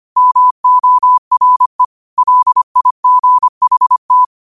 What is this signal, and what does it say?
Morse code, heard as a keyed tone, about 25 wpm, 1000 Hz. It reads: MORE LIGHT
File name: 25wpm